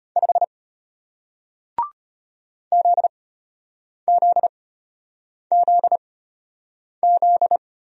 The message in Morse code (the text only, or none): 5 ZZZZ